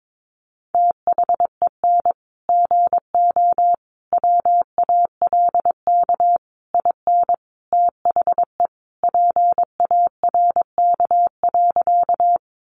Morse code, heard as a keyed tone, about 22 wpm, 700 Hz. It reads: THEN GO WALK IN THE PARK.
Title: Sentences from Top 1300 Words 22wpm